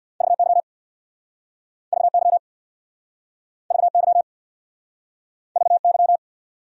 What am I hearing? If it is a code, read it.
Morse code, heard as a keyed tone, about 69 wpm, 700 Hz. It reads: VY VY VY VY